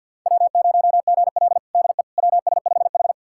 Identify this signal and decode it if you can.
Morse code, heard as a keyed tone, about 49 wpm, 700 Hz. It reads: W0GL DE WS5H
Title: USA Callsign DE Callsign - 4 Characters - 50wpm